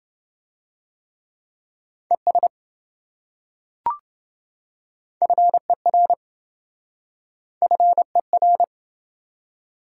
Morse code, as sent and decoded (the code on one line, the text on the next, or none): . ... / ..-. .-. / ..-. .-.
ES FER FER